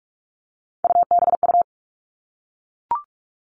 Morse code, heard as a keyed tone, about 44 wpm, 700 Hz. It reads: UBU
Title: All Cut Numbers - 3-Digit 45wpm